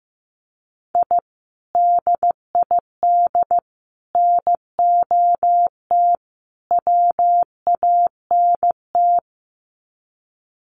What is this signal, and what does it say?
Morse code, heard as a keyed tone, about 15 wpm, 700 Hz. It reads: I DID NOT WANT
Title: Sets of 4 Words 15wpm